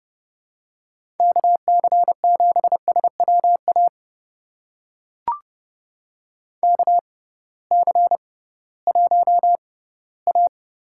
Morse code, heard as a keyed tone, about 31 wpm, spaced as for 18 wpm, 700 Hz. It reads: KC7SWA KC1A
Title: USA Call Signs - 6 Characters - 6x Character Spacing - 30wpm